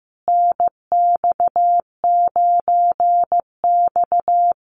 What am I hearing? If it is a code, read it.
Morse code, heard as a keyed tone, about 15 wpm, 700 Hz. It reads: NX9X
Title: Common QSO Phrases Encore 15wpm